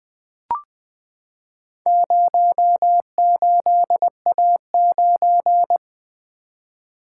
Morse code, heard as a keyed tone, about 20 wpm, 700 Hz. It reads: 08A9
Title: Four Letter-Number Combinations - 20wpm